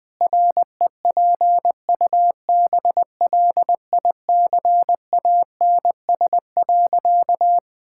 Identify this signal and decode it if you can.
Morse code, heard as a keyed tone, about 20 wpm, 700 Hz. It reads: REPUBLICANS.
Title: Pangram Challenge v2 - 20wpm